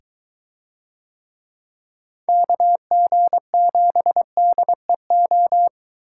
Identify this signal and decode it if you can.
Morse code, heard as a keyed tone, about 21 wpm, 700 Hz. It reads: KG7DEO